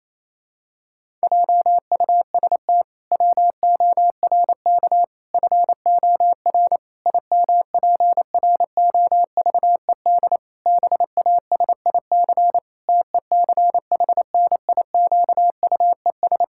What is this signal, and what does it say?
Morse code, heard as a keyed tone, about 28 wpm, 700 Hz. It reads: JUST WORK FOR IMPROVED BASIC TECHNIQUES
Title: Pangram Challenge v2 - 28wpm